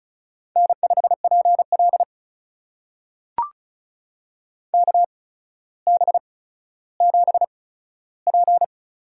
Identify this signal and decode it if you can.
Morse code, heard as a keyed tone, about 36 wpm, spaced as for 15 wpm, 700 Hz. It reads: N5PL KB7P
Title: USA Call Signs - 8x Character Spacing - 35wpm